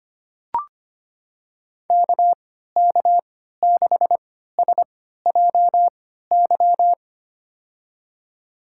Morse code, heard as a keyed tone, about 25 wpm, spaced as for 14 wpm, 700 Hz. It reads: KK6SJY
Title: USA Call Signs - 6 Characters - 3x Character Spacing - 25wpm